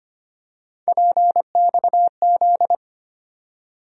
Morse code, heard as a keyed tone, about 25 wpm, 700 Hz. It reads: PXZ
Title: USA Call Sign Postfixes - 3 Letters 25wpm